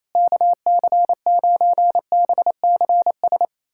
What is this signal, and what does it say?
Morse code, heard as a keyed tone, about 27 wpm, 700 Hz. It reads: KC9BCS
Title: USA Callsign DE Callsign - 6 Characters - 28wpm